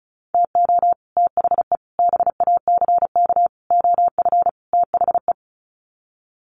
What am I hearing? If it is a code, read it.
Morse code, heard as a keyed tone, about 34 wpm, 700 Hz. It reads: TO THE BACK OF THE
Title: Sets of 5 Words 35wpm